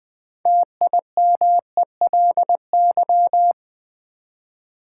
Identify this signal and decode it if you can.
Morse code, heard as a keyed tone, about 20 wpm, 700 Hz. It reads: TIMELY
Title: Build A Word - 3 Letters At A Time - 6-Letter Words - 20wpm